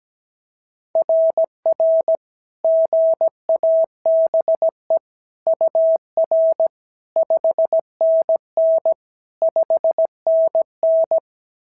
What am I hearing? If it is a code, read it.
Morse code, heard as a keyed tone, about 16 wpm, 650 Hz. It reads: RR GABE UR 5NN 5NN
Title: SOTA QSO Practice 17wpm